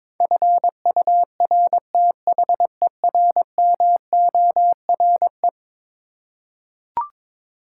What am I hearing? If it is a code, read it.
Morse code, heard as a keyed tone, about 22 wpm, 700 Hz. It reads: FURTHERMORE
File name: practice / 22wpm / Eleven Letter Words - 22wpm